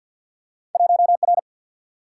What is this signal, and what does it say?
Morse code, heard as a keyed tone, about 52 wpm, 700 Hz. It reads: 1R
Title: Single-Letter Single-Number 50wpm